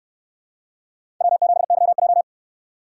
Morse code, heard as a keyed tone, about 68 wpm, 700 Hz. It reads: W7YJ